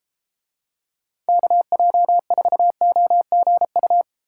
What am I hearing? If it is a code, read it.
Morse code, heard as a keyed tone, about 32 wpm, 700 Hz. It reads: KJ4OGU